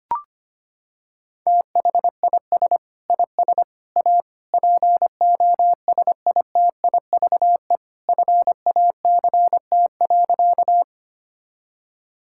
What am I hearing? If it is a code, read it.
Morse code, heard as a keyed tone, about 25 wpm, 700 Hz. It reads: THIS IS A POSITIVE FACT.
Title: Sentences from Top 1000 Words - Review Part 5 - 25wpm